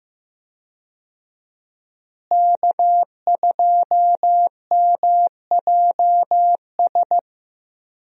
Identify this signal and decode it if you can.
Morse code, heard as a keyed tone, about 15 wpm, 700 Hz. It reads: K2MJS
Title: USA Call Signs - 5 Characters - Repeated 3x - 15wpm